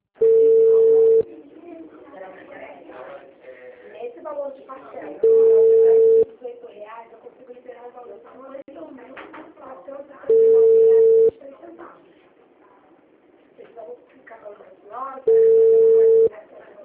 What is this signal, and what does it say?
Ringback tone (China, 450 Hz), with speech.